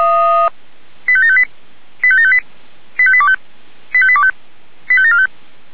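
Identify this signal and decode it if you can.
Five-tone selective call (ZVEI), heard as quick runs of five tones, tones of about 80 ms. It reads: 85758 85758 85825 85825 85745
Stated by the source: ZVEI five-tone call 85827 received on SDR.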